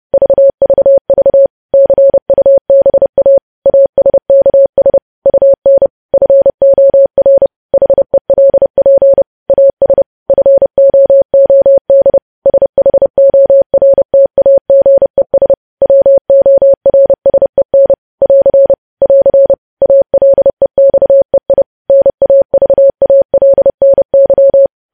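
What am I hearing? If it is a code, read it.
Morse code, heard as a keyed tone, about 30 wpm, 550 Hz. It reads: VVV CUBA ASKS UN FOR HELP AS FOOD SHORTAGES WORSEN <AR> <AR> ALEXEI NAVALNY
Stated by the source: Morse code news headlines sent at 30 words per minute.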